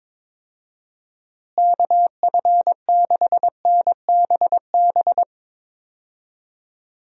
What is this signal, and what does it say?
Morse code, heard as a keyed tone, about 22 wpm, 700 Hz. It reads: KF6NBB
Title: USA Call Signs - 6 Characters - Repeated 3x - 22wpm